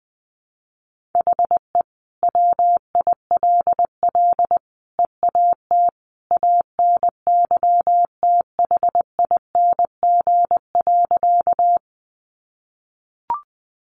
Morse code, heard as a keyed tone, about 20 wpm, 700 Hz. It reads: HE WILL EAT ANYTHING.
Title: Sentences from Top 1000 Words 20wpm